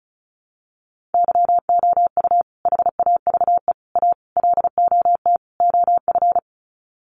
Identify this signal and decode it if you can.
Morse code, heard as a keyed tone, about 34 wpm, 700 Hz. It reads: YOU HAVE A LOT OF
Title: Sets of 5 Words Encore 35wpm